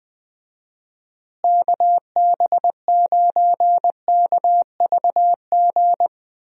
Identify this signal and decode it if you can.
Morse code, heard as a keyed tone, about 20 wpm, 700 Hz. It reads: KB9KVG
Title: USA Call Signs - 6 Characters - Repeated 3x - 20wpm